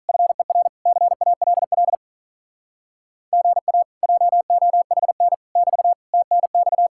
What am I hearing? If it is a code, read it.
Morse code, heard as a keyed tone, about 45 wpm, 700 Hz. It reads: FER CALL GA JOHN <BT> TNX